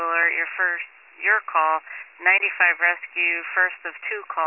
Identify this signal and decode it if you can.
Emergency Alert System (SAME) tones, heard as harsh buzzing bursts, with speech. It is heard from 0.2 to 4.0 s.